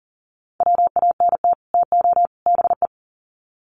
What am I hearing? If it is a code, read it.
Morse code, heard as a keyed tone, about 39 wpm, 700 Hz. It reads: WANT TO BE
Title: Sets of 3 Words 40wpm